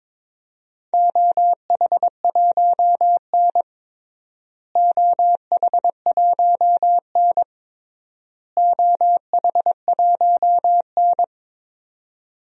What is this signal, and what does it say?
Morse code, heard as a keyed tone, about 22 wpm, 700 Hz. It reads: OH1N OH1N OH1N
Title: DXCC-100 Call Signs - 4 Characters - Repeated 3x - 22wpm